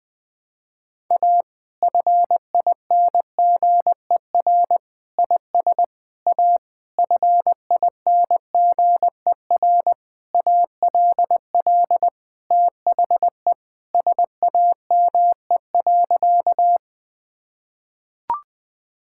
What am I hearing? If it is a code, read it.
Morse code, heard as a keyed tone, about 20 wpm, 700 Hz. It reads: A FINGER IS A FINGER ALL THE SAME.
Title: Sentences from Top 1600 Words 20wpm